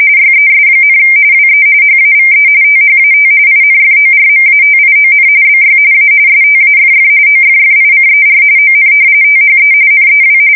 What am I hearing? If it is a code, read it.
RTTY, about 45.45 baud. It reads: YES, WE HAVE THAT INFORMATION BY A STAFF OFFICER ON THE NEW
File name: fsk170 / battleshipnj